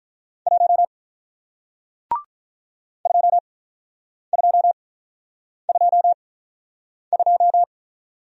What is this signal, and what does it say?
Morse code, heard as a keyed tone, about 48 wpm, spaced as for 9 wpm, 700 Hz. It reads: 1 2222